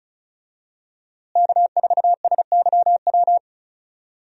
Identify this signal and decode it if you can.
Morse code, heard as a keyed tone, about 34 wpm, 700 Hz. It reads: K4SYW
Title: USA Call Signs - 5 Characters - Repeated 3x - 35wpm